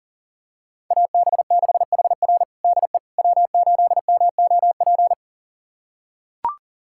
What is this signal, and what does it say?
Morse code, heard as a keyed tone, about 40 wpm, 700 Hz. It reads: AB6HR DE W8MOP
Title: USA Callsign DE Callsign - 5 Characters - 40wpm